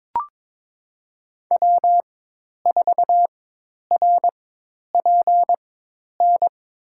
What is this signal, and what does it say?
Morse code, heard as a keyed tone, about 22 wpm, spaced as for 10 wpm, 700 Hz. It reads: W4RPN
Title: USA Call Signs - 5 Characters - 4x Character Spacing - 22wpm